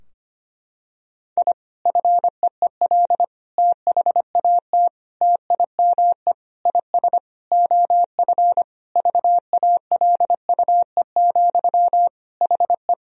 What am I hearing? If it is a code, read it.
Morse code, heard as a keyed tone, about 25 wpm, 700 Hz. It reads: I FEEL THAT TIME IS OF VALUE, HE
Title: Sentences from Top 1000 Words - Review Part 3 - 25wpm